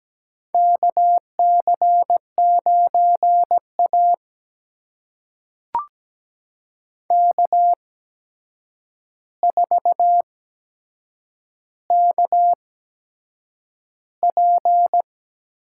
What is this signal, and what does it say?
Morse code, heard as a keyed tone, about 17 wpm, spaced as for 7 wpm, 700 Hz. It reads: KC9A K4KP
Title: USA Call Signs - 4 Characters - 8x Character Spacing - 17wpm